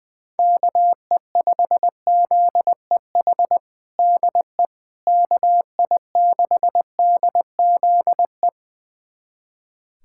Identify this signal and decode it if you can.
Morse code, heard as a keyed tone, about 20 wpm, 700 Hz. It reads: KE5ZEH DE KI6DZE
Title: USA Callsign DE Callsign 20wpm